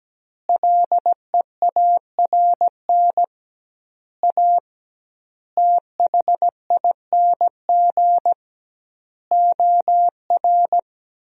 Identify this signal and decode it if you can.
Morse code, heard as a keyed tone, about 17 wpm, 700 Hz. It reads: LEARN A THING OR
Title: 2x Word-Spacing - Sentences from Top 300 Words - 17wpm